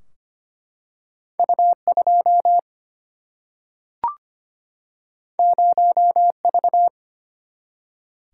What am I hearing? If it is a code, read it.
Morse code, heard as a keyed tone, about 25 wpm, 700 Hz. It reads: U2 0V
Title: Single-Letter Single-Number 25wpm